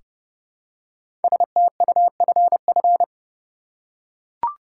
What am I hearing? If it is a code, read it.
Morse code, heard as a keyed tone, about 30 wpm, 700 Hz. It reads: STUFF
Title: Five Letter Words - 30wpm